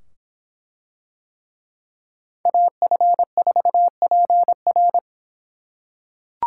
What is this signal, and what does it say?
Morse code, heard as a keyed tone, about 24 wpm, 700 Hz. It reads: AF4PR